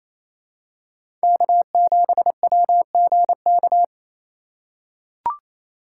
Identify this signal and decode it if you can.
Morse code, heard as a keyed tone, about 28 wpm, 700 Hz. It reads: K7WGK
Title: USA Call Signs - 5 Characters - 2x Character Spacing - 28wpm